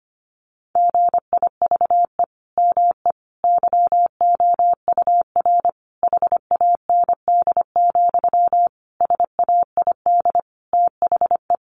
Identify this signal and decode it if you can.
Morse code, heard as a keyed tone, about 25 wpm, 700 Hz. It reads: GIVE ME YOUR HAND, SAID THE
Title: Sentences from Top 1500 Words 25wpm